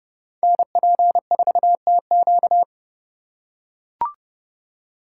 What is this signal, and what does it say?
Morse code, heard as a keyed tone, about 30 wpm, 700 Hz. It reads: NP4TQ